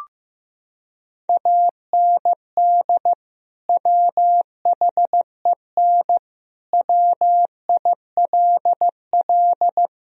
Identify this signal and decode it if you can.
Morse code, heard as a keyed tone, about 15 wpm, 700 Hz. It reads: AND WHEN WILL
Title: Sentences from Top 1000 Words - Review Part 4 - 15wpm